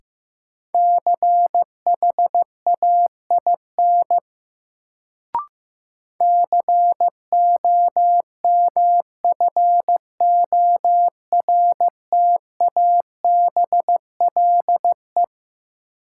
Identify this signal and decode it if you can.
Morse code, heard as a keyed tone, about 15 wpm, 700 Hz. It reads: CHAIN COMFORTABLE
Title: Sentences from Top 1500 Words - New Word Drill Alt - 15wpm